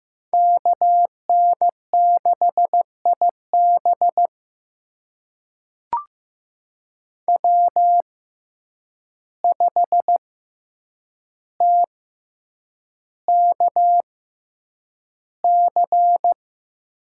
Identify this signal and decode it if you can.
Morse code, heard as a keyed tone, about 15 wpm, spaced as for 8 wpm, 700 Hz. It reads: KN6IB W5TKC
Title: USA Call Signs - 5 Characters - 6x Character Spacing - 15wpm